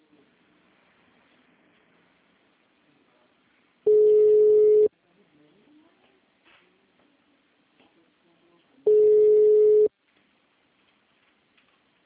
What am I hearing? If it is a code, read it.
Ringback tone (Europe, 425 Hz).